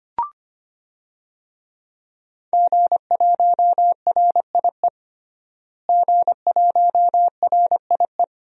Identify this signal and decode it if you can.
Morse code, heard as a keyed tone, about 25 wpm, 700 Hz. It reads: G1RIE G1RIE
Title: DXCC-100 Call Signs - 5 Characters - Repeated 3x - 25wpm